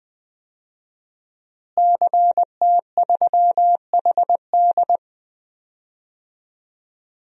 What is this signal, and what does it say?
Morse code, heard as a keyed tone, about 21 wpm, 700 Hz. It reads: CT3HD
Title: DXCC-100 Call Signs - 5 Characters - Repeated 2x - 20wpm